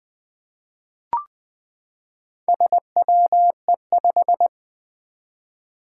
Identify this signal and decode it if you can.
Morse code, heard as a keyed tone, about 20 wpm, 700 Hz. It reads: SWE5
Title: Four Challenging Letter-Number Combinations 20wpm